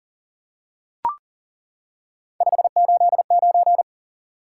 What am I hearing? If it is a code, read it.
Morse code, heard as a keyed tone, about 40 wpm, 700 Hz. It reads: 589
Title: Top 180 QSO Elements 40wpm